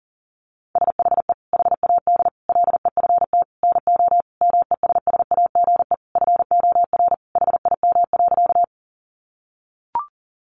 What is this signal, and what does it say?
Morse code, heard as a keyed tone, about 40 wpm, 700 Hz. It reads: SHE HAD LEFT NO MESSAGE FOR HIM.
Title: Sentences from Top 1100 Words 40wpm